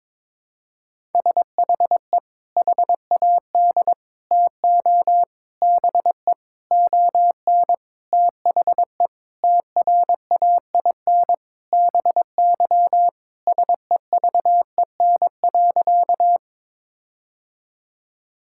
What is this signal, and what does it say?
Morse code, heard as a keyed tone, about 22 wpm, 700 Hz. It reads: SHE HAD TO BE ON THE TRAIN BY SEVEN.
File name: practice / 22wpm / Sentences with 4-Word Phrases - 22wpm